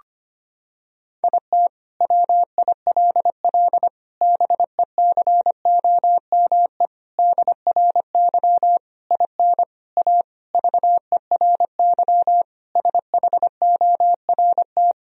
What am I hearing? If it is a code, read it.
Morse code, heard as a keyed tone, about 25 wpm, 700 Hz. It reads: IT WILL BECOME DRY IN A VERY SHORT
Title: Sentences from Top 1000 Words - Review Part 3 - 25wpm